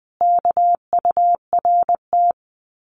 Morse code, heard as a keyed tone, about 20 wpm, 700 Hz. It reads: KURT
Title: Common QSO Phrases Encore 20wpm